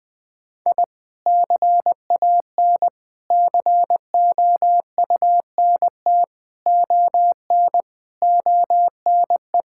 Morse code, heard as a keyed tone, about 20 wpm, 700 Hz. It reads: I CAN COUNT ON ONE
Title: Sentences with 8-Word Phrases - Spoken Beginning - 20wpm